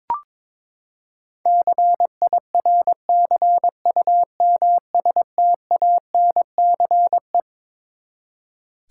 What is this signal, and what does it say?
Morse code, heard as a keyed tone, about 22 wpm, 700 Hz. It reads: CIRCUMSTANCE
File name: Twelve Letter Words - 22wpm